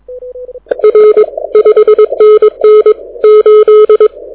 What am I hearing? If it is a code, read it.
Morse code, heard as a keyed tone, about 22 wpm, 430 Hz. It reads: R 5NN 8
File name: audio zawody / IARU 2015 CW SN7Q